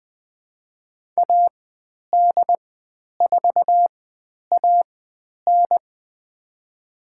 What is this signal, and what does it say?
Morse code, heard as a keyed tone, about 20 wpm, spaced as for 10 wpm, 700 Hz. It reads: AD4AN
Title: 102 Lesson 17g - Letter u - Call Signs 20wpm